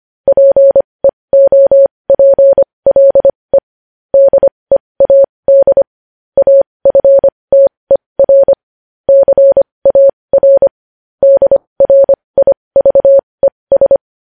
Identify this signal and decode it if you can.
Morse code, heard as a keyed tone, about 25 wpm, spaced as for 20 wpm, 550 Hz. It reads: PEOPLE DEAD AFTER CAR DRIVES
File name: podcast / News headlines-1x-25x20